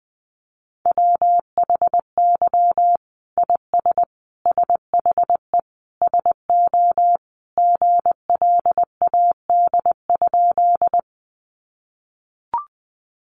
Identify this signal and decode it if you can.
Morse code, heard as a keyed tone, about 20 wpm, 700 Hz. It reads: WHY IS SHE SO GLAD?
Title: Sentences from Top 800 Words 20wpm